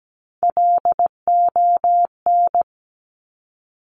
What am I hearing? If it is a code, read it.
Morse code, heard as a keyed tone, about 17 wpm, 700 Hz. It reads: LON
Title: Top Three Letter Combinations in Words 17wpm